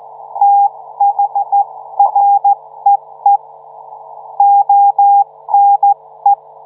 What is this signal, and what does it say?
Morse code, heard as a keyed tone, about 15 wpm, 800 Hz. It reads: THREE ONE